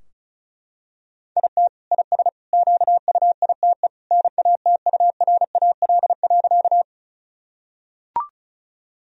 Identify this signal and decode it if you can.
Morse code, heard as a keyed tone, about 34 wpm, 700 Hz. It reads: IT IS QUITE NATURAL.
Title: Sentences from Top 1000 Words - Review Part 4 - 35wpm